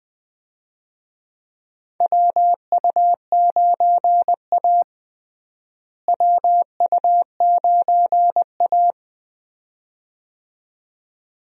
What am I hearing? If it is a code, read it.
Morse code, heard as a keyed tone, about 21 wpm, 700 Hz. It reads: WU9A WU9A
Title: USA Call Signs - 4 Characters - Repeated 2x - 20wpm